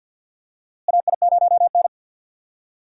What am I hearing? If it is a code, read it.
Morse code, heard as a keyed tone, about 52 wpm, 700 Hz. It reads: AI0N
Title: USA Call Signs - 4 Characters - Repeated 2x - 50wpm